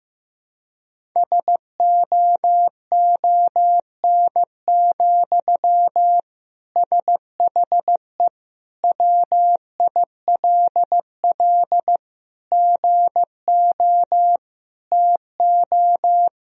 Morse code, heard as a keyed tone, about 15 wpm, 700 Hz. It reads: SOON, SHE WILL GO TO
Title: Sentences from Top 1000 Words - Review Part 1 - 15wpm